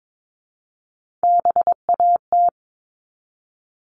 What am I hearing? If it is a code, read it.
Morse code, heard as a keyed tone, about 22 wpm, 700 Hz. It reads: BAT
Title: Top 1251-1500 Words 22wpm